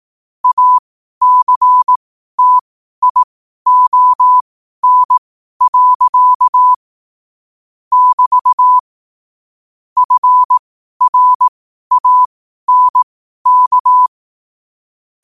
Morse code, heard as a keyed tone, about 18 wpm, spaced as for 12 wpm, 1000 Hz. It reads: ACTION. <BT> FRANK
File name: qotdfiles / 12wpm